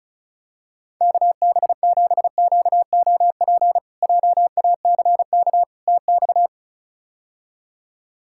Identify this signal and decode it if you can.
Morse code, heard as a keyed tone, about 36 wpm, 700 Hz. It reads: KB7QOP JACK TX
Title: K1USN SST QSO Practice - 35wpm